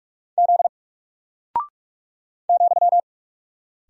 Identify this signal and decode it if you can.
Morse code, heard as a keyed tone, about 44 wpm, 700 Hz. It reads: Z ,
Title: Single Character (Letter:Number:Punctuation:Prosign) Alt 45wpm